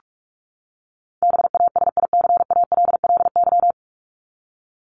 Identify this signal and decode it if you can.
Morse code, heard as a keyed tone, about 44 wpm, 700 Hz. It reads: BASICALLY
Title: Top 1251-1500 Words 45wpm